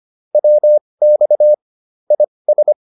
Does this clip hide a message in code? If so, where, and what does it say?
Yes: Morse code from 0.3 to 2.7 s, heard as a keyed tone, about 25 wpm, spaced as for 20 wpm, 600 Hz. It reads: WX IS